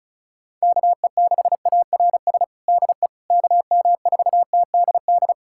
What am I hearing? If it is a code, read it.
Morse code, heard as a keyed tone, about 36 wpm, 700 Hz. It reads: KE6ARS DE KM4TDD
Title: USA Callsign DE Callsign - 6 Characters - 35wpm